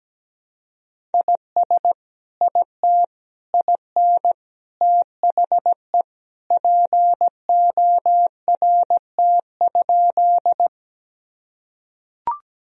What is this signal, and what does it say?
Morse code, heard as a keyed tone, about 17 wpm, 700 Hz. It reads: IS IT IN THE PORT?
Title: Sentences from Top 1000 Words - Review Part 1 - 17wpm